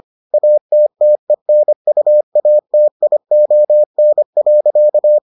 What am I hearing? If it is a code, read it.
Morse code, heard as a keyed tone, about 25 wpm, 600 Hz. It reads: ATTENUATION.